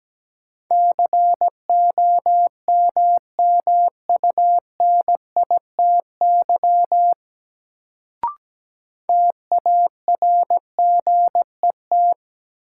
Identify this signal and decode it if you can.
Morse code, heard as a keyed tone, about 17 wpm, 700 Hz. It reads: COMMUNITY TARGET
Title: Sentences from Top 1200 Words - New Word Drill Alt - 17wpm